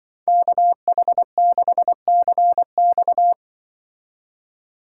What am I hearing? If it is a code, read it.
Morse code, heard as a keyed tone, about 24 wpm, 700 Hz. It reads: KH6CX
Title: Speed Racing - USA 2x2 Call Signs 20wpm